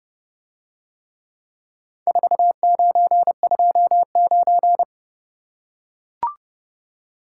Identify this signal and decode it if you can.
Morse code, heard as a keyed tone, about 30 wpm, 700 Hz. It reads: 4929